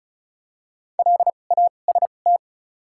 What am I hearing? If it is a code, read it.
Morse code, heard as a keyed tone, about 34 wpm, spaced as for 25 wpm, 700 Hz. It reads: LAST
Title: 2x Character Spacing - Top 300 Words - 35wpm